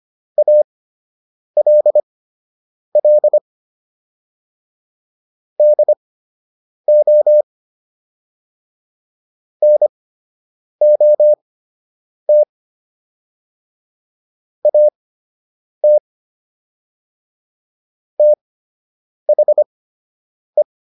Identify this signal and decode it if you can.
Morse code, heard as a keyed tone, about 25 wpm, spaced as for 8 wpm, 600 Hz. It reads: ALL DO NOT AT THE